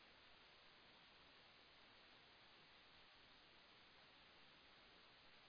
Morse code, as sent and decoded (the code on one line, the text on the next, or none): none